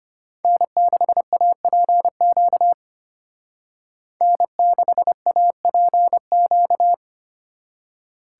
Morse code, heard as a keyed tone, about 30 wpm, 700 Hz. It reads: N6APQ N6APQ
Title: Speed Racing - USA 1x3 Call Signs 25wpm